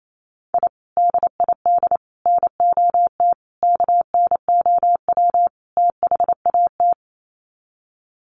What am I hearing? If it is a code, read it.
Morse code, heard as a keyed tone, about 28 wpm, 700 Hz. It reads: I DID NOT KNOW THAT
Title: Sets of 5 Words 28wpm